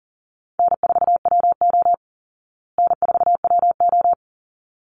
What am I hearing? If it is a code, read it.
Morse code, heard as a keyed tone, about 40 wpm, 700 Hz. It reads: N4WO N4WO
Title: Call Signs Repeated 3x - 40wpm